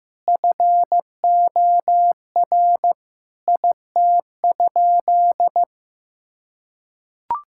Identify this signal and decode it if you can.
Morse code, heard as a keyed tone, about 15 wpm, 700 Hz. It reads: FOR IT?
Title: Sentences from Top 1000 Words - Review Part 6 - 15wpm